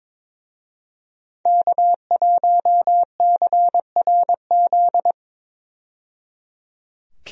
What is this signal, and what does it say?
Morse code, heard as a keyed tone, about 21 wpm, 700 Hz. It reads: K1CRZ
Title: USA Call Signs - 5 Characters - 22wpm